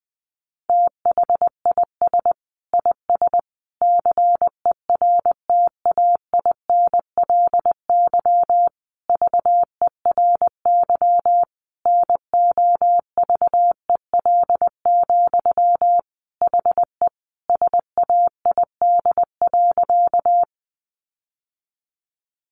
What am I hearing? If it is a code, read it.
Morse code, heard as a keyed tone, about 19 wpm, 700 Hz. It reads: THIS IS CERTAINLY VERY NOVEL, HE SAID.
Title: Sentences from Top 1300 Words 20wpm